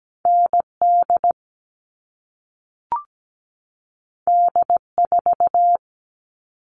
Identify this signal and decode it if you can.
Morse code, heard as a keyed tone, about 17 wpm, 700 Hz. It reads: ND D4
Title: All Cut Numbers - 2-Digit 17wpm